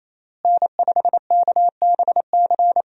Morse code, heard as a keyed tone, about 28 wpm, 700 Hz. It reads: N5KBC
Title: USA Call Signs - 5 Characters - Repeated 3x - 28wpm